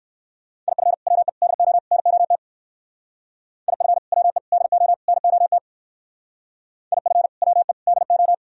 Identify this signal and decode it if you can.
Morse code, heard as a keyed tone, about 55 wpm, 700 Hz. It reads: IF WE DO NOT IF WE DO NOT IF WE DO
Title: Speed-Racing - Sets of 4 Words 45wpm